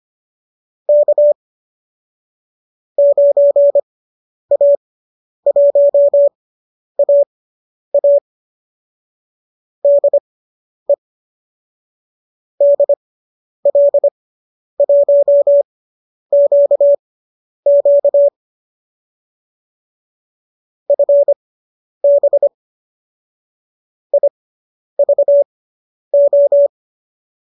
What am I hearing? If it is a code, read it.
Morse code, heard as a keyed tone, about 25 wpm, spaced as for 10 wpm, 570 Hz. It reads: K 9A1AA DE DL1QQ FB IVO
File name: qso219 10